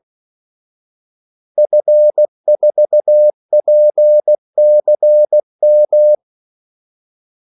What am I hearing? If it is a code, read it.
Morse code, heard as a keyed tone, about 16 wpm, 600 Hz. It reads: F4PCM
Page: Kaldesignaler 16 wpm